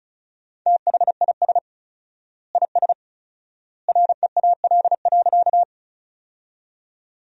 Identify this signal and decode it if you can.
Morse code, heard as a keyed tone, about 36 wpm, 700 Hz. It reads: THIS IS REAL.
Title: 4x Word-Spacing - Sentences from Top 300 Words - 35wpm